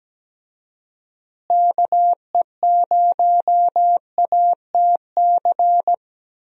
Morse code, heard as a keyed tone, about 17 wpm, 700 Hz. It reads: KE0ATC
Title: USA Call Signs - 6 Characters - Repeated 3x - 17wpm